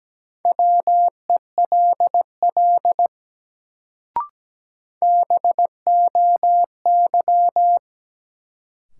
Morse code, heard as a keyed tone, about 17 wpm, 700 Hz. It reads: WELL BOY
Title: Sentences from Top 200 Words - New Word Drill - 17wpm